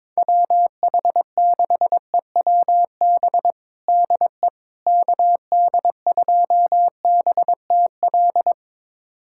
Morse code, heard as a keyed tone, about 21 wpm, 700 Hz. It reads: WH6EWB DE KD2BTL
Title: USA Callsign DE Callsign - 6 Characters - 22wpm